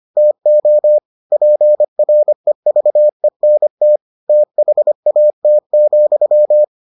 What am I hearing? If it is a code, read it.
Morse code, heard as a keyed tone, about 25 wpm, 600 Hz. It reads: TO PREVENT THAT,